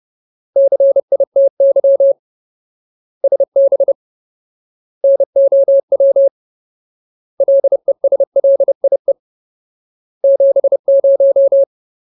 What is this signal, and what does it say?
Morse code, heard as a keyed tone, about 30 wpm, 550 Hz. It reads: CITY SB NOW LESLIE 70
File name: Ultimate Word List / 30WPM Ultimate